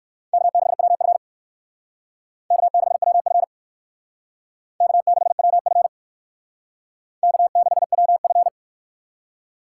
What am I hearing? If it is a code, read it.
Morse code, heard as a keyed tone, about 59 wpm, 700 Hz. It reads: K6WF K6WF K6WF K6WF